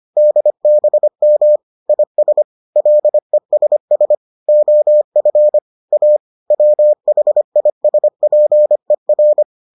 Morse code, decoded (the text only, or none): DBM IS LESS OF A WHISPER